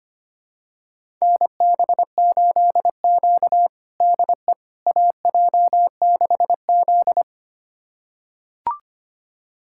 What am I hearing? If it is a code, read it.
Morse code, heard as a keyed tone, about 25 wpm, 700 Hz. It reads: NB8Q DE AJ6Z
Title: USA Callsign DE Callsign - 4 Characters - 25wpm